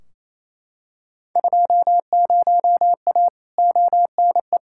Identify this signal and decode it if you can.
Morse code, heard as a keyed tone, about 28 wpm, 700 Hz. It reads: 20A ONE
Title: Field Day Exchanges Encore - 28wpm